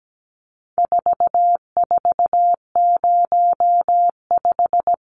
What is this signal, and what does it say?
Morse code, heard as a keyed tone, about 17 wpm, 700 Hz. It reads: 4405
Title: Numbers - 4-Digit 17wpm